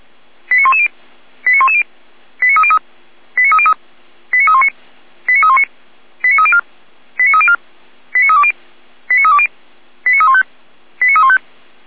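Five-tone selective call (ZVEI), heard as quick runs of five tones, tones of about 70 ms. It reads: 89110 89110 89303 89303 89319 89319 89304 89304 89320 89320 89315 89315
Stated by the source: Five-tone ZVEI selective call received on SDR.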